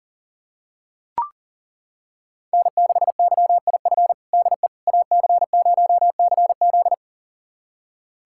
Morse code, heard as a keyed tone, about 40 wpm, 700 Hz. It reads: N6YIF DE AC0CZ
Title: USA Callsign DE Callsign - 5 Characters - 40wpm